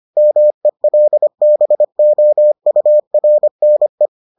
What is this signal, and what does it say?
Morse code, heard as a keyed tone, about 25 wpm, 600 Hz. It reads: MELBOURNE